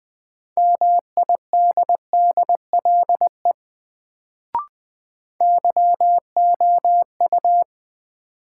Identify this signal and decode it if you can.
Morse code, heard as a keyed tone, about 21 wpm, 700 Hz. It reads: MIDDLE YOU
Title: Sets of 2 Words - New Word Drill - 20wpm